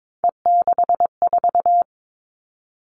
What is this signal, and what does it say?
Morse code, heard as a keyed tone, about 22 wpm, 700 Hz. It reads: E64
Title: Three Challenging Letter-Number Combinations 22wpm